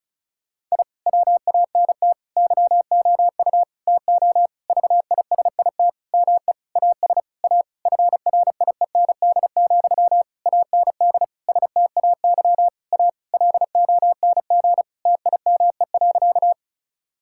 Morse code, heard as a keyed tone, about 36 wpm, 700 Hz. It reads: I WANT YOU TO VISIT ME AS A FRIEND, AND STAY A LONG TIME.
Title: Sentences from Top 1000 Words - Review Part 6 - 35wpm